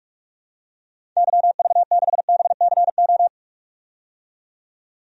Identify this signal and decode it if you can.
Morse code, heard as a keyed tone, about 46 wpm, 700 Hz. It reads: YV6BCO
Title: DXCC-100 Call Signs - 6 Characters - 45wpm